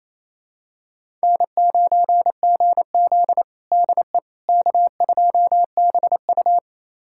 Morse code, heard as a keyed tone, about 28 wpm, 700 Hz. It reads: N9GZ DE K2BU
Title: USA Callsign DE Callsign - 4 Characters - 28wpm